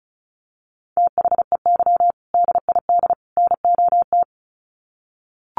Morse code, heard as a keyed tone, about 34 wpm, 700 Hz. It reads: THEY DID NOT
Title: Sets of 3 Words 35wpm